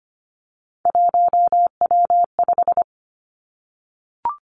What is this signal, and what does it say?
Morse code, heard as a keyed tone, about 25 wpm, 700 Hz. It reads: 1W5
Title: Three Challenging Letter-Number Combinations 25wpm